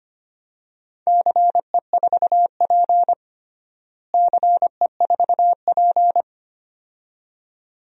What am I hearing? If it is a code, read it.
Morse code, heard as a keyed tone, about 25 wpm, 700 Hz. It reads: CE4P CE4P
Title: DXCC-100 Call Signs - 4 Characters - Repeated 2x - 25wpm